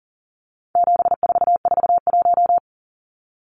Morse code, heard as a keyed tone, about 40 wpm, 700 Hz. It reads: 7441
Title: Numbers - 4-Digit 40wpm